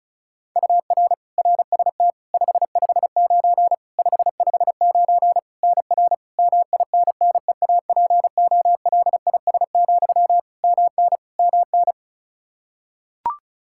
Morse code, heard as a keyed tone, about 36 wpm, 700 Hz. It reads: UR RST 559 559 NR MINNEAPOLIS, MN MN
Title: Practice QSO - Exchange 1 - Beginning - 35wpm